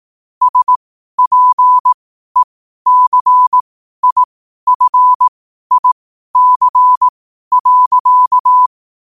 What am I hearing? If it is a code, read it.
Morse code, heard as a keyed tone, about 18 wpm, spaced as for 12 wpm, 1000 Hz. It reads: SPECIFIC.
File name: qotdfiles / 12wpm